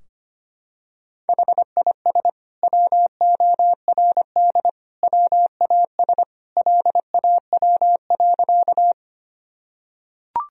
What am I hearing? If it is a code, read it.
Morse code, heard as a keyed tone, about 25 wpm, 700 Hz. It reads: HIS WORD WAS LAW.
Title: Sentences from Top 1000 Words - Review Part 3 - 25wpm